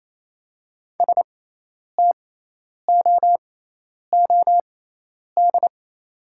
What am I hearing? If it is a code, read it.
Morse code, heard as a keyed tone, about 28 wpm, spaced as for 10 wpm, 700 Hz. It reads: STOOD
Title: 6x Character Spacing - Top 500 Words - 28wpm